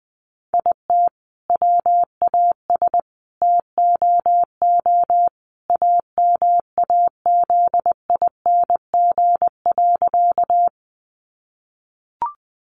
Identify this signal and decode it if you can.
Morse code, heard as a keyed tone, about 19 wpm, 700 Hz. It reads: IT WAS TOO AMAZING.
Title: Sentences from Top 1500 Words 20wpm